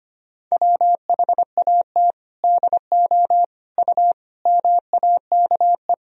Morse code, heard as a keyed tone, about 25 wpm, 700 Hz. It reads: WHAT DO U MAKE
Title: Sentences from Top 100 Words and Abbreviations 25wpm